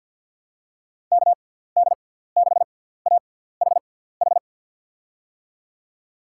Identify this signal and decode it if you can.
Morse code, heard as a keyed tone, about 50 wpm, spaced as for 17 wpm, 700 Hz. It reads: KD6AHH
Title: USA Call Signs - 6 Characters - 6x Character Spacing - 50wpm